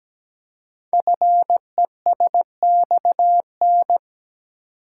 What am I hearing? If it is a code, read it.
Morse code, heard as a keyed tone, about 17 wpm, 700 Hz. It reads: FESXN
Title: Code Groups - 5 Letters - 17wpm